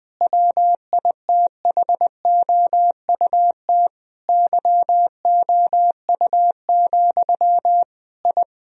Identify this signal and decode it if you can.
Morse code, heard as a keyed tone, about 20 wpm, 700 Hz. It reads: WITHOUT YOU, I
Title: Sentences from Top 1000 Words - Review Part 3 - 20wpm